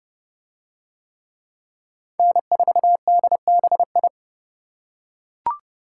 Morse code, heard as a keyed tone, about 29 wpm, 700 Hz. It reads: N4DBI